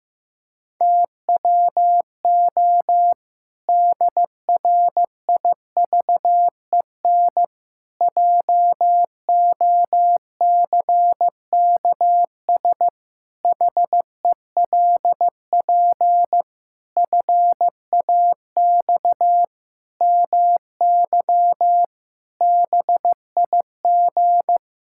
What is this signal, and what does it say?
Morse code, heard as a keyed tone, about 15 wpm, 700 Hz. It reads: TWO DRIVEN JOCKS HELP FAX MY BIG
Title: Pangram Challenge v2 - 15wpm